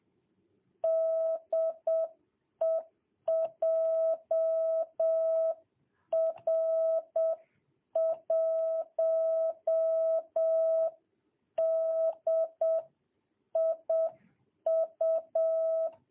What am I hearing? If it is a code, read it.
Morse code, heard as a keyed tone, about 7 wpm, 660 Hz. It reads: DE JR1DIU
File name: jh8cbh / 2022cw